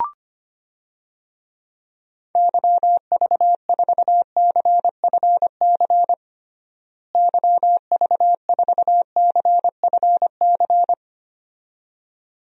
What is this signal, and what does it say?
Morse code, heard as a keyed tone, about 25 wpm, 700 Hz. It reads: YV4CFC YV4CFC
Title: DXCC-100 Call Signs - 6 Characters - Repeated 2x - 25wpm